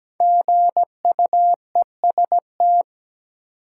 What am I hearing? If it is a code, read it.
Morse code, heard as a keyed tone, about 17 wpm, 700 Hz. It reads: GUEST
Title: Sentences from Top 1600 Words - New Word Drill - 17wpm